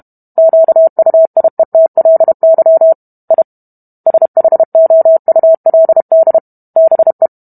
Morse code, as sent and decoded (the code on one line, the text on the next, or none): --.- ..- .. - .-.. -.-- / .. / ... .... --- ..- .-.. -.. / -... .
QUIETLY I SHOULD BE